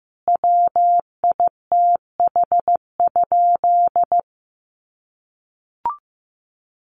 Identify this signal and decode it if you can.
Morse code, heard as a keyed tone, about 15 wpm, 700 Hz. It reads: WITH?
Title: Sentences from Top 1400 Words 15wpm